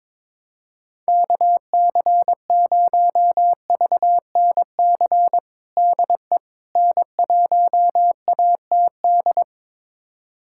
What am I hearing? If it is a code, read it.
Morse code, heard as a keyed tone, about 22 wpm, 700 Hz. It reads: KC0VNC DE N1ATD
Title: USA Callsign DE Callsign 22wpm